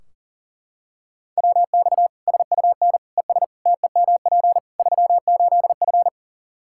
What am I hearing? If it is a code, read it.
Morse code, heard as a keyed tone, about 40 wpm, 700 Hz. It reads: WX SUN ES TEMP 38F
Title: Practice QSO - Exchange 2 - Info - 40wpm